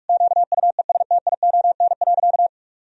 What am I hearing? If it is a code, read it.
Morse code, heard as a keyed tone, about 46 wpm, 700 Hz. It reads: QUESTION.
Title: Sentences from Top 1000 Words - Review Part 2 - 45wpm